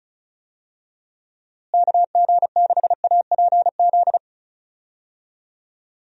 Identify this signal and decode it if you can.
Morse code, heard as a keyed tone, about 36 wpm, 700 Hz. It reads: KG6APZ
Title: USA Call Signs - 6 Characters - Repeated 3x - 35wpm